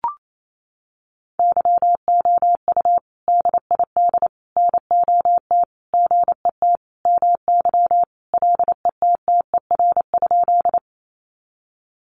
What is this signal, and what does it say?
Morse code, heard as a keyed tone, about 28 wpm, 700 Hz. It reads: YOU DID NOT GET MY LETTER?
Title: Sentences from Top 600 Words 28wpm